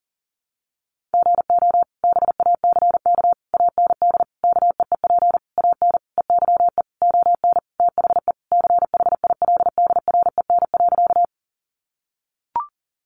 Morse code, heard as a keyed tone, about 40 wpm, 700 Hz. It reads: GO BACK AND KEEP AN EYE ON THE CHILDREN.
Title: Sentences from Top 300 Words 40wpm